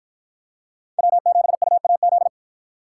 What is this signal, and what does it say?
Morse code, heard as a keyed tone, about 53 wpm, 700 Hz. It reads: W7UAZ